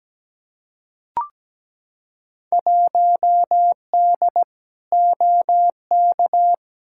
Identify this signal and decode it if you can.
Morse code, heard as a keyed tone, about 17 wpm, 700 Hz. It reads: 1D OK
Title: Field Day Exchanges - 17wpm